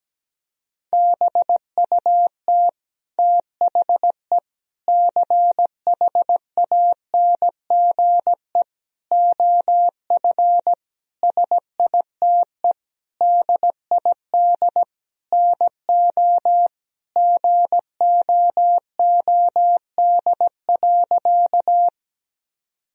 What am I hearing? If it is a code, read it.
Morse code, heard as a keyed tone, about 17 wpm, 700 Hz. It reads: BUT THE CHANGE OF SITE DID NO GOOD.
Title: Sentences from Top 1000 Words - Review Part 4 - 17wpm